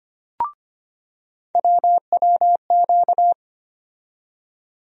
Morse code, heard as a keyed tone, about 25 wpm, 700 Hz. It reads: WWQ
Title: USA Call Sign Postfixes - 3 Letters 25wpm